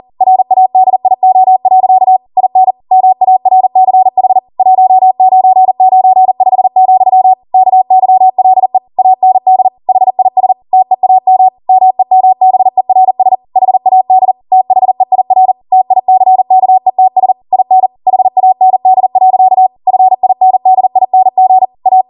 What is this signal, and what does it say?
Morse code, heard as a keyed tone, about 40 wpm, 750 Hz. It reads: RADIO. IN MARCH 1995, KYLE AND HIS TEAM MEMBERS HAD THEIR TICKETS IN HAND. FINDING A